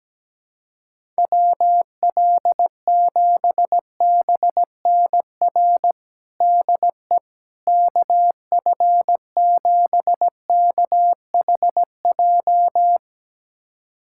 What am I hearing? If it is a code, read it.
Morse code, heard as a keyed tone, about 17 wpm, 700 Hz. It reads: WL7BNR DE KF7KHJ
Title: USA Callsign DE Callsign - 6 Characters - 17wpm